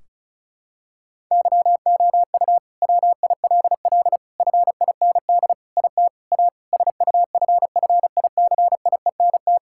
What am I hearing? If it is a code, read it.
Morse code, heard as a keyed tone, about 36 wpm, 700 Hz. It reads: YOU WILL FIND IT A SUFFICIENT
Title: Sentences from Top 1600 Words 35wpm